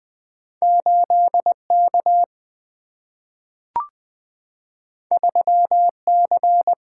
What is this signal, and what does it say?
Morse code, heard as a keyed tone, about 20 wpm, 700 Hz. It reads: 8K 3C
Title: Two Letter-Number Combinations - 20wpm